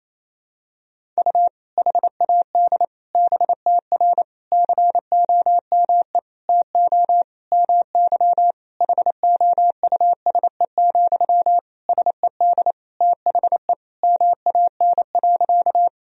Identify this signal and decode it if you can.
Morse code, heard as a keyed tone, about 28 wpm, 700 Hz. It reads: U HAD BTR COME TO MY HOUSE, SED THE MAN.
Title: Sentences from Top 400 Words and Abbreviations 28wpm